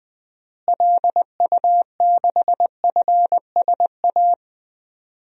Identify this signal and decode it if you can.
Morse code, heard as a keyed tone, about 20 wpm, 700 Hz. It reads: LU6FSA
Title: DXCC-100 Call Signs - 6 Characters - Repeated 3x - 20wpm